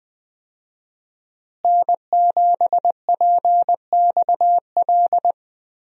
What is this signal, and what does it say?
Morse code, heard as a keyed tone, about 20 wpm, 700 Hz. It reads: N7PXL
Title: USA Call Signs - 5 Characters - Repeated 3x - 20wpm